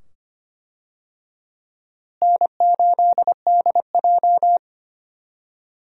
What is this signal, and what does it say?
Morse code, heard as a keyed tone, about 25 wpm, 700 Hz. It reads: N8DJ
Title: USA Call Signs - 4 Characters - Repeated 2x - 25wpm